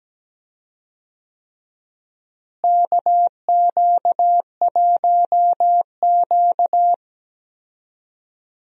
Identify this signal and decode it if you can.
Morse code, heard as a keyed tone, about 17 wpm, 700 Hz. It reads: KQ1Q
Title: USA Call Signs - 4 Characters - Repeated 3x - 17wpm